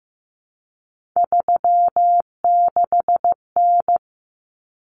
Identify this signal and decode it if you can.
Morse code, heard as a keyed tone, about 15 wpm, 700 Hz. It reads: 36N
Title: Common Cut Numbers - 3-Digit 15wpm